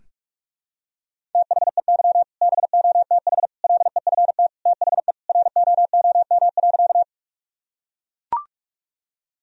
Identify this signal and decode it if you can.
Morse code, heard as a keyed tone, about 42 wpm, 700 Hz. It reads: THEY BOTH LEFT THE ROOM.
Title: Sentences from Top 1000 Words - Review Part 1 - 45wpm